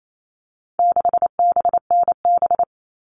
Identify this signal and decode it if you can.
Morse code, heard as a keyed tone, about 28 wpm, 700 Hz. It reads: 6BNB
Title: All Cut Numbers - 4-Digit 28wpm